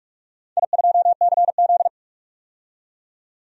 Morse code, heard as a keyed tone, about 44 wpm, 700 Hz. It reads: I2CZ
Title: DXCC-100 Call Signs - 4 Characters - 45wpm